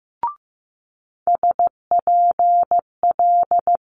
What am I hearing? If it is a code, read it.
Morse code, heard as a keyed tone, about 15 wpm, 700 Hz. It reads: SPL
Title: Multi-letter Phonemes Quick Fire 15wpm